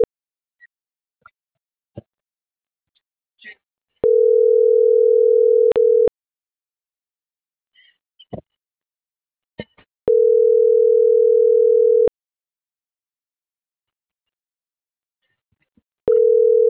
Ringback tone (North America), 440 + 480 Hz.
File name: CENTRAL SAGOT / out adriana.poncec